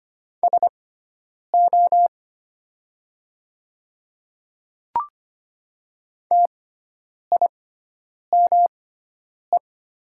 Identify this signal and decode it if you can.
Morse code, heard as a keyed tone, about 25 wpm, spaced as for 9 wpm, 700 Hz. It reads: SO TIME
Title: 6x Character Spacing - Top 500 Words - 25wpm